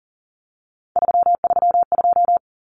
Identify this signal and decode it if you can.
Morse code, heard as a keyed tone, about 40 wpm, 700 Hz. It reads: 332
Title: Common Cut Numbers - 3-Digit 40wpm